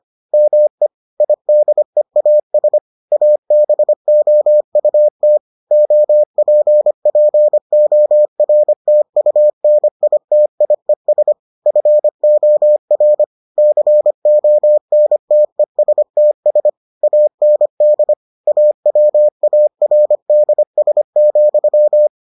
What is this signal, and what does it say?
Morse code, heard as a keyed tone, about 25 wpm, 600 Hz. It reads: ME IDEAS ABOUT OPPORTUNITIES FOR CONTESTS AND AWARDS,